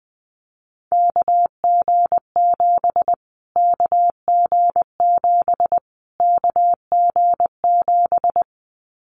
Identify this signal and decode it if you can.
Morse code, heard as a keyed tone, about 20 wpm, 700 Hz. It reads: KG7 KG7 KG7
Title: Callsign Prefixes for Top 25 DXCC - Repeated 3x - 20wpm